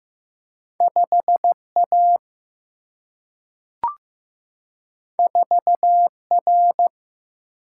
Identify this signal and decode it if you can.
Morse code, heard as a keyed tone, about 15 wpm, 700 Hz. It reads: 5 A 4 R
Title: Single-Letter Single-Number 15wpm